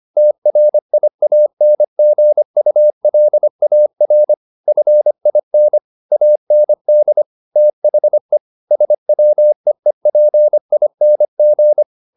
Morse code, heard as a keyed tone, about 25 wpm, 600 Hz. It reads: TRIANGULAR FIN AND THE SWEEPING